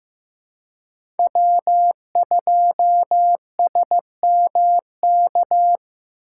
Morse code, heard as a keyed tone, about 15 wpm, 700 Hz. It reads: W2SMK
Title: USA Call Signs - 5 Characters - 6x Character Spacing - 15wpm